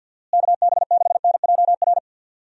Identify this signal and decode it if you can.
Morse code, heard as a keyed tone, about 52 wpm, 700 Hz. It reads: KB6NJR
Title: USA Call Signs - 6 Characters - Repeated 3x - 50wpm